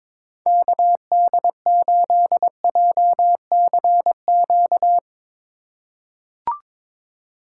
Morse code, heard as a keyed tone, about 23 wpm, 700 Hz. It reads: KD8JCQ
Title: USA Call Signs - 6 Characters - 3x Character Spacing - 22wpm